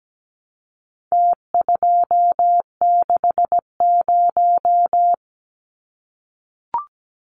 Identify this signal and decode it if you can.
Morse code, heard as a keyed tone, about 17 wpm, 700 Hz. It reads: T260
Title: Numbers - Contest Style - 4-Digit 17wpm